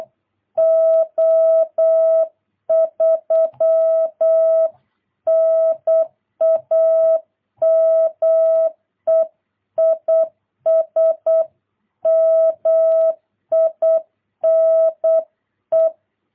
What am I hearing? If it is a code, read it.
Morse code, heard as a keyed tone, about 8 wpm, 650 Hz. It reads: O3 NAME IS MINE
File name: jh8cbh / 2022cw